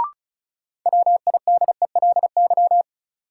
Morse code, heard as a keyed tone, about 36 wpm, 700 Hz. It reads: WIDELY
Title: Sentences from Top 1200 Words - New Word Drill - 35wpm